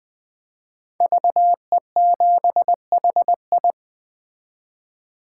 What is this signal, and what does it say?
Morse code, heard as a keyed tone, about 20 wpm, 700 Hz. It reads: VE7HI
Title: 2023 SOTA Chaser Call Signs - 20wpm